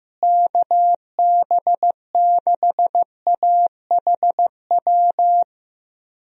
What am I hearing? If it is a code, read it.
Morse code, heard as a keyed tone, about 16 wpm, 700 Hz. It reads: KB6AHW
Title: USA Call Signs - 6 Characters - Repeated 3x - 15wpm